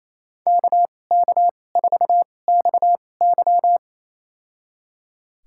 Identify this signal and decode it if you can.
Morse code, heard as a keyed tone, about 28 wpm, spaced as for 20 wpm, 700 Hz. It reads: KK4XY
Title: USA Call Signs - 2x Character Spacing - 28wpm